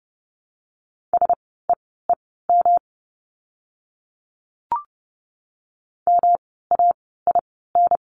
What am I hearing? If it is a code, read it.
Morse code, heard as a keyed tone, about 30 wpm, spaced as for 17 wpm, 700 Hz. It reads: SEEM MAIN
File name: course / 30:10wpm / 108 Lesson 18f - Letter m - Words 30wpm